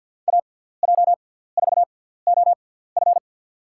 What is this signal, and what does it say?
Morse code, heard as a keyed tone, about 53 wpm, 700 Hz. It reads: A J 4 O F
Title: USA Call Signs - 5 Characters - 6x Character Spacing - 50wpm